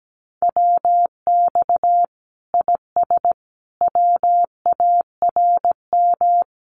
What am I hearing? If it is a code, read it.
Morse code, heard as a keyed tone, about 17 wpm, 700 Hz. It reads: WX IS WARM
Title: Common QSO Phrases 17wpm